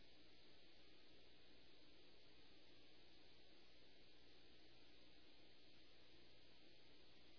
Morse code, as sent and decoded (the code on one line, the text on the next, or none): none